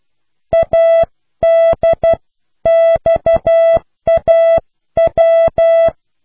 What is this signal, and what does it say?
Morse code, heard as a keyed tone, about 12 wpm, 660 Hz. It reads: ADXAW